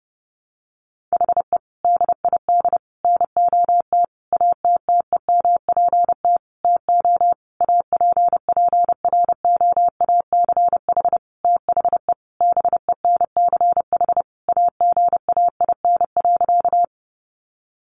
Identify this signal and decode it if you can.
Morse code, heard as a keyed tone, about 30 wpm, 700 Hz. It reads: HE DID NOT ATTEMPT TO APPROACH THE BENCH AGAIN.
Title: Sentences from Top 1400 Words 30wpm